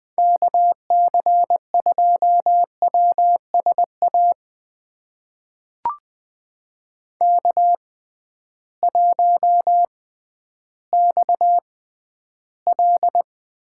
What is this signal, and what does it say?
Morse code, heard as a keyed tone, about 21 wpm, spaced as for 12 wpm, 700 Hz. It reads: KC2WSA K1XL
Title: USA Call Signs - 6x Character Spacing - 20wpm